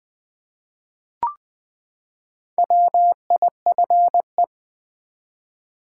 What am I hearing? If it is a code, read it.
Morse code, heard as a keyed tone, about 20 wpm, 700 Hz. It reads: WIFE
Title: Four Letter Words - 20wpm